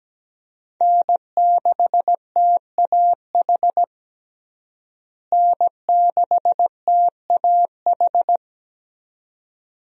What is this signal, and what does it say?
Morse code, heard as a keyed tone, about 17 wpm, 700 Hz. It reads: N6TAH N6TAH
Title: USA Call Signs - 5 Characters - Repeated 2x - 17wpm